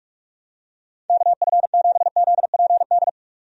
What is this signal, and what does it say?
Morse code, heard as a keyed tone, about 47 wpm, 700 Hz. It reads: KF7ZPD
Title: USA Call Signs - 6 Characters - 45wpm